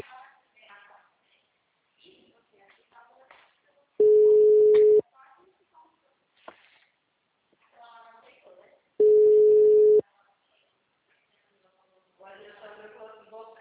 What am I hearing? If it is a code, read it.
Ringback tone (Europe, 425 Hz), with speech.
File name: out- -2024- - - .118